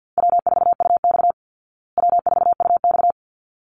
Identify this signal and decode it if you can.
Morse code, heard as a keyed tone, about 49 wpm, 700 Hz. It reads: W4UX W4UX
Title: Easy Call Signs Repeated 2x - 50wpm